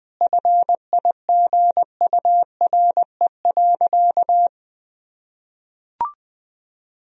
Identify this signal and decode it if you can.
Morse code, heard as a keyed tone, about 20 wpm, 700 Hz. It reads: FIGURE.
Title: Sentences from Top 1000 Words - Review Part 2 - 20wpm